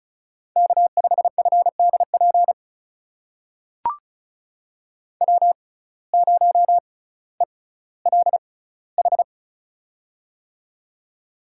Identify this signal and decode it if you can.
Morse code, heard as a keyed tone, about 36 wpm, spaced as for 18 wpm, 700 Hz. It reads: K5FDP W0ELH
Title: USA Call Signs - 6x Character Spacing - 35wpm